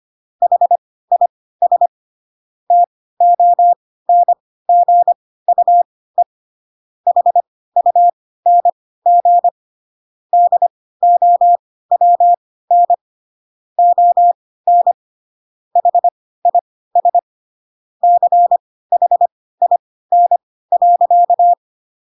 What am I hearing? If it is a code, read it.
Morse code, heard as a keyed tone, about 25 wpm, spaced as for 16 wpm, 700 Hz. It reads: HIS TONGUE HUNG DOWN ON HIS CHIN.
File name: Pinocchio / Pinocchio ch1 10 25 ik1hge